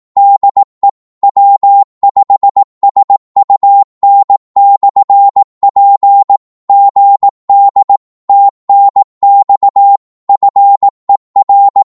Morse code, heard as a keyed tone, about 18 wpm, 800 Hz. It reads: DE W5SUN/P GD TNX FER
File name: Morse / qso-18wpm